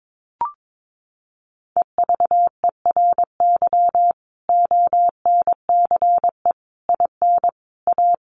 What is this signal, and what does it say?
Morse code, heard as a keyed tone, about 22 wpm, 700 Hz. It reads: EVERY ONCE IN A
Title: Sets of 5 Words 22wpm